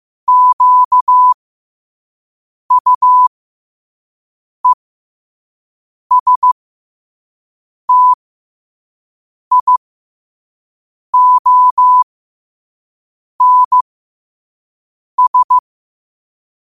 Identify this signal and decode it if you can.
Morse code, heard as a keyed tone, about 15 wpm, spaced as for 5 wpm, 1000 Hz. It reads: QUESTIONS
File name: qotdfiles / 5wpm